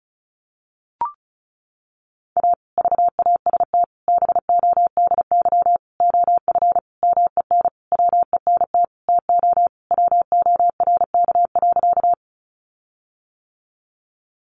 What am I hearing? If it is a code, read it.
Morse code, heard as a keyed tone, about 34 wpm, 700 Hz. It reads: A VAST BODY OF MEN WENT TO WORK.
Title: Sentences from Top 1300 Words 35wpm